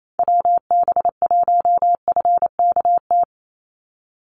Morse code, heard as a keyed tone, about 28 wpm, 700 Hz. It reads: WB1FKT
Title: USA Call Signs 28wpm